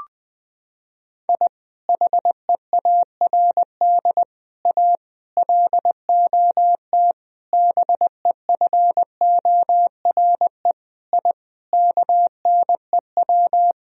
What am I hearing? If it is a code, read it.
Morse code, heard as a keyed tone, about 21 wpm, 700 Hz. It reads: I HEARD A LOT BEFORE I KNEW
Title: Sentences from Top 1000 Words - Review Part 3 - 20wpm